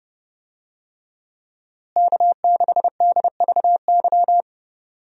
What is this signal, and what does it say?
Morse code, heard as a keyed tone, about 30 wpm, 700 Hz. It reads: K6DVY
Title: USA Call Signs - 5 Characters - Repeated 3x - 30wpm